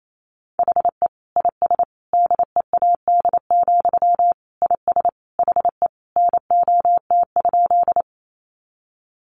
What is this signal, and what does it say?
Morse code, heard as a keyed tone, about 28 wpm, 700 Hz. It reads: HE IS DEAD, IS HE NOT?
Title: Sentences from Top 1500 Words 28wpm